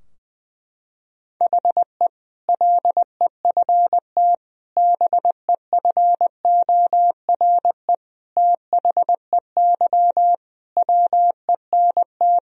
Morse code, heard as a keyed tone, about 21 wpm, 700 Hz. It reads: HE LEFT BEFORE THEY WENT
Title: Sentences from Top 1000 Words - Review Part 4 - 20wpm